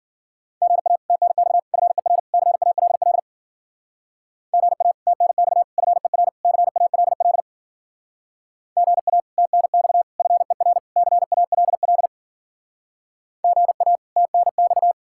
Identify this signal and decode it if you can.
Morse code, heard as a keyed tone, about 49 wpm, 700 Hz. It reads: GA TNX FER CALL GA TNX FER CALL GA TNX FER CALL GA TNX
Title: Speed-Racing Common QSO Phrases - 40wpm